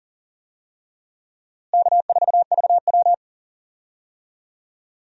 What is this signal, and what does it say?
Morse code, heard as a keyed tone, about 40 wpm, 700 Hz. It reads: K4VW
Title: USA Call Signs - 4 Characters - Repeated 2x - 40wpm